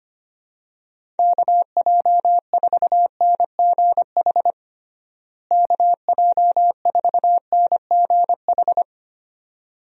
Morse code, heard as a keyed tone, about 25 wpm, 700 Hz. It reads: KJ4NGH KJ4NGH
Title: USA Call Signs - 6 Characters - Repeated 2x - 25wpm